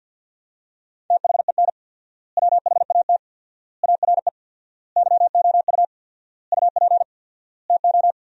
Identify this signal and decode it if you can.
Morse code, heard as a keyed tone, about 50 wpm, 700 Hz. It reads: THEN WHAT ARE YOU UP TO
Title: 4x Word-Spacing - Sentences from Top 100 Words - 50wpm